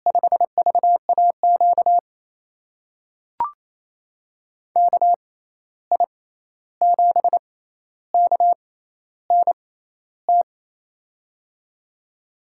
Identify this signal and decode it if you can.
Morse code, heard as a keyed tone, about 27 wpm, spaced as for 13 wpm, 700 Hz. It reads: 5VAQ KI7KNT
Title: USA Call Signs - 6 Characters - 6x Character Spacing - 28wpm